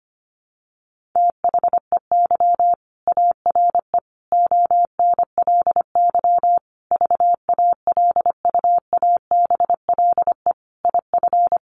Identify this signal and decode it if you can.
Morse code, heard as a keyed tone, about 25 wpm, 700 Hz. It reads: THEY ARE ONLY VALUABLE IF
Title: Sentences from Top 1400 Words 25wpm